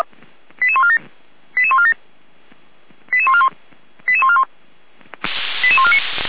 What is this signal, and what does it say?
Five-tone selective call (ZVEI), heard as quick runs of five tones, tones of about 70 ms. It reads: 88147 88147 88141 88141 88149
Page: /455/SDR/ZVEI/88437/2024.06.30/